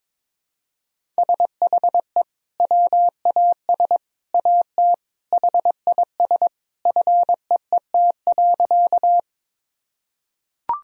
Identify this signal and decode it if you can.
Morse code, heard as a keyed tone, about 21 wpm, 700 Hz. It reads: SHE WAS AT HIS FEET.
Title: Sentences from Top 1000 Words - Review Part 1 - 22wpm